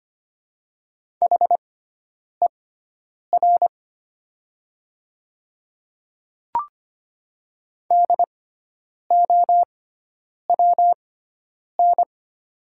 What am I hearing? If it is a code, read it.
Morse code, heard as a keyed tone, about 25 wpm, spaced as for 9 wpm, 700 Hz. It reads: HER DOWN
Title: 6x Character Spacing - Top 100 Words - 25wpm